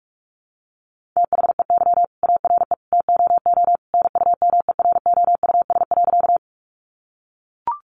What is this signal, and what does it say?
Morse code, heard as a keyed tone, about 44 wpm, 700 Hz. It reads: THEY ARE TOO NUMEROUS.
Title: Sentences from Top 1400 Words 45wpm